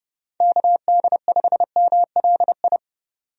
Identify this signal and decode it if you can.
Morse code, heard as a keyed tone, about 30 wpm, 700 Hz. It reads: KD5MLI
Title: USA Call Signs - 6 Characters - 30wpm